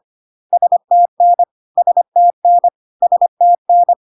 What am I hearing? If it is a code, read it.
Morse code, heard as a keyed tone, about 25 wpm, 700 Hz. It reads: STN STN STN
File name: CW abbreviations alphabetical repeated 3 times 25 25wpm